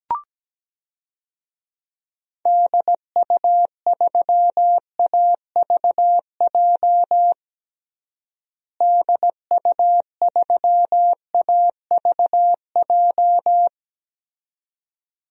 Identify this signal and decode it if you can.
Morse code, heard as a keyed tone, about 17 wpm, 700 Hz. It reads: DU3AVJ DU3AVJ
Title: DXCC-100 Call Signs - 6 Characters - Repeated 2x - 17wpm